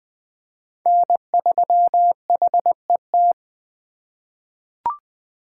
Morse code, heard as a keyed tone, about 20 wpm, 700 Hz. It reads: N3HET
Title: USA Call Signs - 5 Characters - 3x Character Spacing - 20wpm